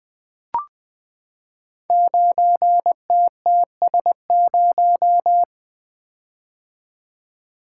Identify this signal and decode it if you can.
Morse code, heard as a keyed tone, about 20 wpm, 700 Hz. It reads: 9TTS0
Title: Code Groups - 5 Letter:Numbers - 20wpm